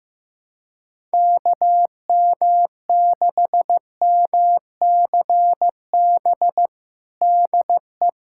Morse code, heard as a keyed tone, about 15 wpm, 700 Hz. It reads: KM6MCB DE
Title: USA Callsign DE Callsign 15wpm